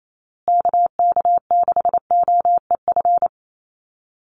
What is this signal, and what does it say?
Morse code, heard as a keyed tone, about 28 wpm, 700 Hz. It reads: KK6OEF
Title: USA Call Signs Repeated 3x 28wpm